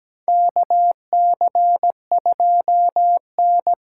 Morse code, heard as a keyed tone, about 17 wpm, 700 Hz. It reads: KC2N
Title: USA Callsign DE Callsign - 4 Characters - 17wpm